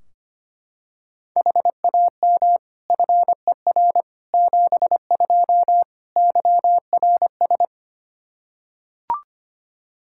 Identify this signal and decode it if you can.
Morse code, heard as a keyed tone, about 25 wpm, 700 Hz. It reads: HAM FER 72 YRS
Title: K1USN SST QSO Practice 25wpm